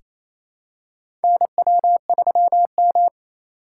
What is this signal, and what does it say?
Morse code, heard as a keyed tone, about 28 wpm, 700 Hz. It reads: NW3M
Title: USA Call Signs - 4 Characters - 4x Character Spacing - 28wpm